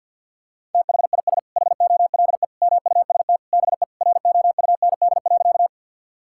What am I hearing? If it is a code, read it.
Morse code, heard as a keyed tone, about 46 wpm, 700 Hz. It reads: THIS HOLE MUST BE ROUND.
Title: Sentences from Top 1600 Words 50wpm